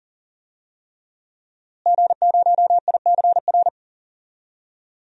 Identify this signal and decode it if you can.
Morse code, heard as a keyed tone, about 40 wpm, 700 Hz. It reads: G0ICR
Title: DXCC-100 Call Signs - 5 Characters - Repeated 3x - 40wpm